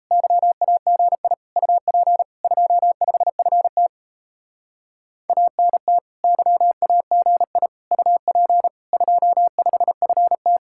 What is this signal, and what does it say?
Morse code, heard as a keyed tone, about 34 wpm, 700 Hz. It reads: YAGI UP 25FT ANT YAGI UP 25FT
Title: Speed-Racing Common QSO Phrases - 28wpm